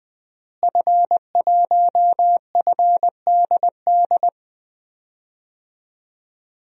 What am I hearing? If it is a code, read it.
Morse code, heard as a keyed tone, about 20 wpm, 700 Hz. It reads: F1FDD
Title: DXCC-100 Call Signs - 5 Characters - Repeated 2x - 20wpm